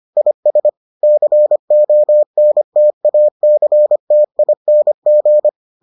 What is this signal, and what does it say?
Morse code, heard as a keyed tone, about 25 wpm, 600 Hz. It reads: IS CONTACTING